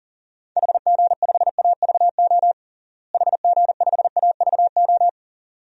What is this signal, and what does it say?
Morse code, heard as a keyed tone, about 40 wpm, 700 Hz. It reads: HG5AVO HG5AVO
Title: DXCC-100 Call Signs - 6 Characters - Repeated 2x - 40wpm